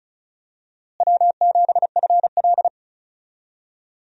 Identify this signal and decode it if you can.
Morse code, heard as a keyed tone, about 36 wpm, 700 Hz. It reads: W7FL
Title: USA Call Signs - 4 Characters - Repeated 2x - 35wpm